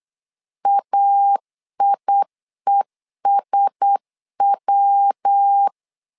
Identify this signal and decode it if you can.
Morse code, heard as a keyed tone, about 8 wpm, 800 Hz. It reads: AIESW
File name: LEZIONI / 4 Q1S 8wpm 2710